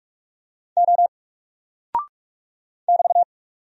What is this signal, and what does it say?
Morse code, heard as a keyed tone, about 44 wpm, 700 Hz. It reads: O <BT>
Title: Single Character (Letter:Number:Punctuation:Prosign) Alt 45wpm